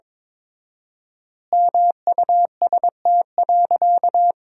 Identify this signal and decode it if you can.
Morse code, heard as a keyed tone, about 22 wpm, 700 Hz. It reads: MUST.
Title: 4x Word-Spacing - Sentences from Top 200 Words - 22wpm